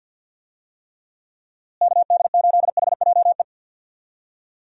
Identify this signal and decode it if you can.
Morse code, heard as a keyed tone, about 49 wpm, 700 Hz. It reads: KD8HJE
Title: USA Call Signs - 6 Characters - Repeated 2x - 50wpm